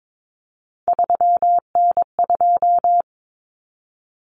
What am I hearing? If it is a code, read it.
Morse code, heard as a keyed tone, about 22 wpm, 700 Hz. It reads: 3N2
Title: Common Cut Numbers - 3-Digit 22wpm